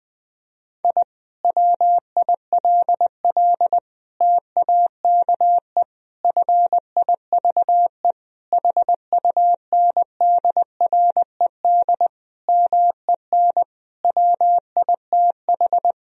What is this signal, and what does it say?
Morse code, heard as a keyed tone, about 20 wpm, 700 Hz. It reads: I WILL TAKE FIVE HUNDRED MEN WITH
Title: Sentences from Top 1000 Words - Review Part 2 - 20wpm